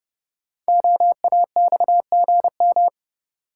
Morse code, heard as a keyed tone, about 30 wpm, 690 Hz. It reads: OAXGM
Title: Code Groups - 5 Letters - 30wpm